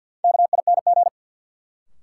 Morse code, heard as a keyed tone, about 53 wpm, 700 Hz. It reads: KING
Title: Sentences from Top 400 Words - New Word Drill - 50wpm